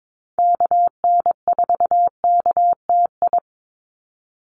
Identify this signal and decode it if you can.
Morse code, heard as a keyed tone, about 22 wpm, 700 Hz. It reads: KN4KTI
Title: USA Call Signs 22wpm